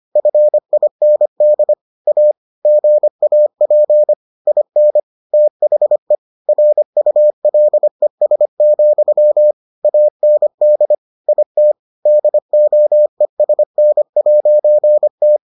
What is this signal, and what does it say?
Morse code, heard as a keyed tone, about 25 wpm, 600 Hz. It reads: FIND A GAP IN THE RULES, AND IT DOESN'T